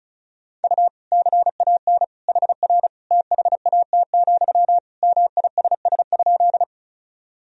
Morse code, heard as a keyed tone, about 34 wpm, 700 Hz. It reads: U CAN HR THAT, MISS?
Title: Sentences from Top 500 Words and Abbreviations 35wpm